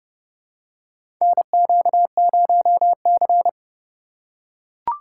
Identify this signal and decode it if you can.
Morse code, heard as a keyed tone, about 30 wpm, 700 Hz. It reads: NQ0C
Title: USA Call Signs - 2x Character Spacing - 30wpm